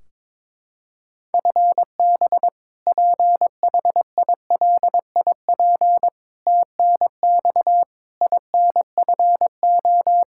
Morse code, heard as a keyed tone, about 22 wpm, 700 Hz. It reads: FB PHILIP TNX INFO
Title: Practice QSO - Exchange 3 - More Info - 22wpm